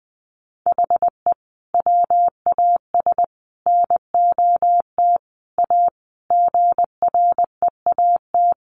Morse code, heard as a keyed tone, about 20 wpm, 700 Hz. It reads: HE WAS NOT A GREAT
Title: Sentences from Top 1200 Words 20wpm